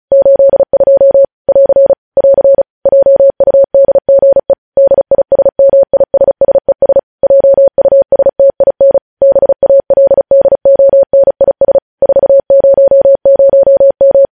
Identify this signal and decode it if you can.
Morse code, heard as a keyed tone, about 36 wpm, 550 Hz. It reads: 82 <AR> <AR> JUDGE DISMISSES JUSTIN BALDONIS 400M
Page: news headlines in morse code at 35 wpm